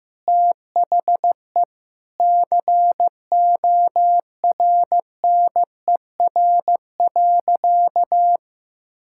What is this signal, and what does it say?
Morse code, heard as a keyed tone, about 15 wpm, 700 Hz. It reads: THE CORNER.
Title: Sentences from Top 1600 Words 15wpm